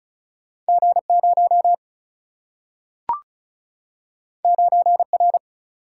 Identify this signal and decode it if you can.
Morse code, heard as a keyed tone, about 36 wpm, 700 Hz. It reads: G 0 9 R
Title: Single-Letter Single-Number 35wpm